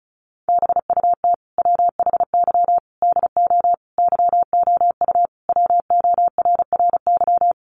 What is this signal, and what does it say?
Morse code, heard as a keyed tone, about 34 wpm, 700 Hz. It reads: BUT WHY DO YOU WORRY
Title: Sentences from Top 1100 Words 35wpm